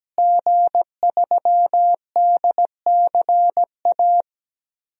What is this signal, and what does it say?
Morse code, heard as a keyed tone, about 17 wpm, 700 Hz. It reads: G3DCA
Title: DXCC-100 Call Signs - 5 Characters - Repeated 3x - 17wpm